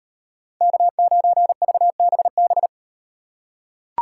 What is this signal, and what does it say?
Morse code, heard as a keyed tone, about 36 wpm, 700 Hz. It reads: K9VBB